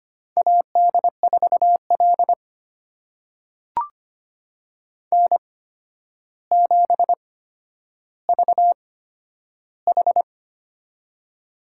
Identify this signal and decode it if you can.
Morse code, heard as a keyed tone, about 25 wpm, spaced as for 11 wpm, 700 Hz. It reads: AD4L N7VH
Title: USA Call Signs - 4 Characters - 8x Character Spacing - 25wpm